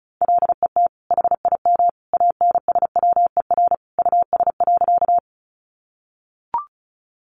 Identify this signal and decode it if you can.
Morse code, heard as a keyed tone, about 34 wpm, 700 Hz. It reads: LET HIM ANSWER US.
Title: Sentences from Top 300 Words 35wpm